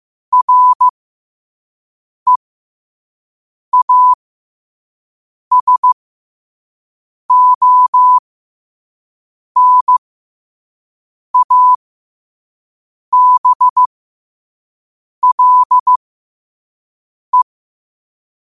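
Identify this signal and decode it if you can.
Morse code, heard as a keyed tone, about 15 wpm, spaced as for 5 wpm, 1000 Hz. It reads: REASONABLE
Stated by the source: Morse code quote-of-the-day practice at 5 wpm.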